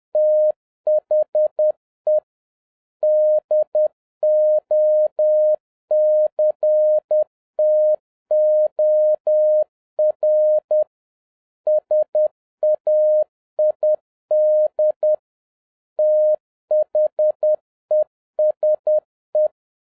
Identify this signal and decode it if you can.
Morse code, heard as a keyed tone, about 10 wpm, 610 Hz. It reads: THE DOCTOR SAID THESE
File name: 2011-07-mp3 / n56 BACK FROM A BIG ACCIDENT 10wpm